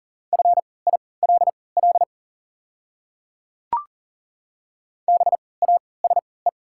Morse code, heard as a keyed tone, about 40 wpm, spaced as for 23 wpm, 700 Hz. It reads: FILL BASE
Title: 3x Character Spacing - Top 500 Words - 40wpm